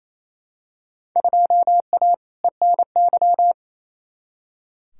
Morse code, heard as a keyed tone, about 28 wpm, 700 Hz. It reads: 2A ENY
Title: Field Day Exchanges Encore - 28wpm